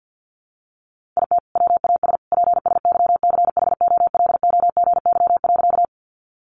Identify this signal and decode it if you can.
Morse code, heard as a keyed tone, about 50 wpm, 700 Hz. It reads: IT WAS PSYCHOLOGY.
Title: Sentences from Top 1400 Words 50wpm